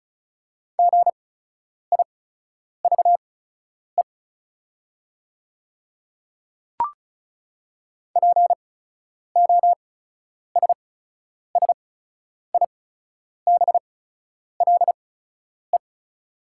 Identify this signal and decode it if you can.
Morse code, heard as a keyed tone, about 34 wpm, spaced as for 10 wpm, 700 Hz. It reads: GIVE POSSIBLE
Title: 8x Character Spacing - Top 500 Words - 35wpm